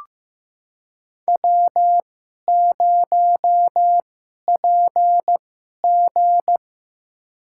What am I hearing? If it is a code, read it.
Morse code, heard as a keyed tone, about 15 wpm, spaced as for 11 wpm, 700 Hz. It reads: W0PG
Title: USA Call Signs - 2x Character Spacing - 15wpm